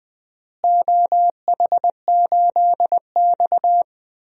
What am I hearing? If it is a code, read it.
Morse code, heard as a keyed tone, about 20 wpm, 700 Hz. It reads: OH8X
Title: DXCC-100 Call Signs - 4 Characters - Repeated 2x - 20wpm